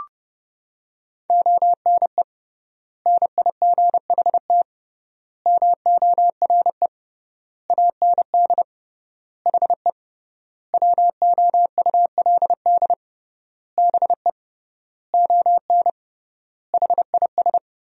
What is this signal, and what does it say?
Morse code, heard as a keyed tone, about 32 wpm, 700 Hz. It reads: ONE NIGHT MORE AND HE WOULD BE ON HIS
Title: 3x Word-Spacing - Sentences from Top 300 Words - 30wpm